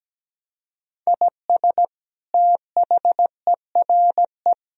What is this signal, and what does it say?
Morse code, heard as a keyed tone, about 17 wpm, 700 Hz. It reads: IS THERE
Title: Sentences from Top 1000 Words - Review Part 4 - 17wpm